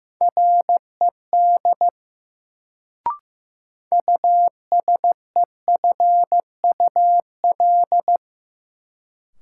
Morse code, heard as a keyed tone, about 15 wpm, 700 Hz. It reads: RED USEFUL
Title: Sentences from Top 700 Words - New Word Drill - 15wpm